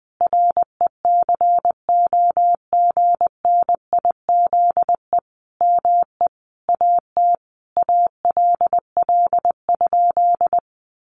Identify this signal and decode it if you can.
Morse code, heard as a keyed tone, about 20 wpm, 700 Hz. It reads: RECOGNIZE ME AT ALL?
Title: Sentences from Top 1400 Words 20wpm